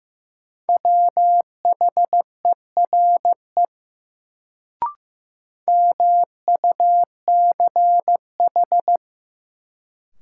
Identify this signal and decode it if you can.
Morse code, heard as a keyed tone, about 15 wpm, 700 Hz. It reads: WHERE MUCH
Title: Sets of 2 Words - New Word Drill - 15wpm